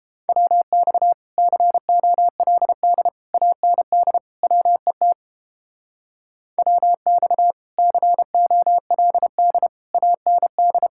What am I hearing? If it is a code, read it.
Morse code, heard as a keyed tone, about 29 wpm, 700 Hz. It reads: WX COLD AND WET WX COLD AND